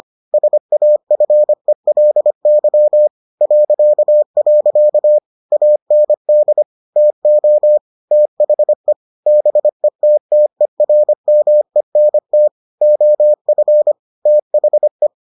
Morse code, heard as a keyed tone, about 25 wpm, 600 Hz. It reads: SAFELY .. AND TO THE BETTERMENT OF THE